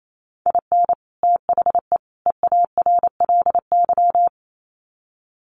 Morse code, heard as a keyed tone, about 28 wpm, 700 Hz. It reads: IN THE EARLY
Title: Sets of 3 Words 28wpm